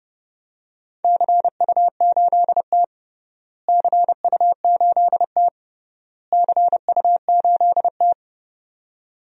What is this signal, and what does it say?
Morse code, heard as a keyed tone, about 30 wpm, 700 Hz. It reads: CU8T CU8T CU8T
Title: DXCC-100 Call Signs - 4 Characters - Repeated 3x - 30wpm